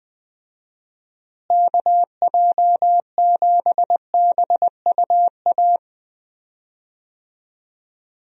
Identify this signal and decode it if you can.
Morse code, heard as a keyed tone, about 21 wpm, 700 Hz. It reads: KJ7BUA
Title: USA Call Signs - 6 Characters - Repeated 3x - 20wpm